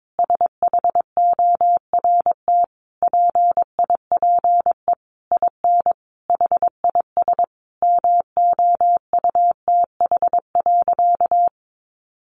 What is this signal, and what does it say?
Morse code, heard as a keyed tone, about 22 wpm, 700 Hz. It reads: SHORT PIPE IN HIS MOUTH.
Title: Sentences from Top 1500 Words 22wpm